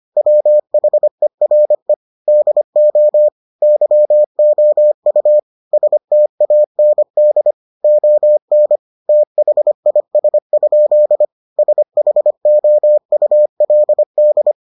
Morse code, heard as a keyed tone, about 25 wpm, 600 Hz. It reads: WHERE DO YOU STAND ON THIS? SHOULD